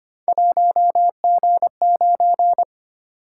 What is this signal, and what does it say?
Morse code, heard as a keyed tone, about 25 wpm, 700 Hz. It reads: 1G9
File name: Three Letter-Number Combinations - 25wpm